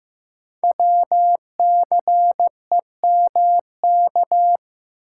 Morse code, heard as a keyed tone, about 15 wpm, 700 Hz. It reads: WCEMK
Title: Code Groups - 5 Letters - 15wpm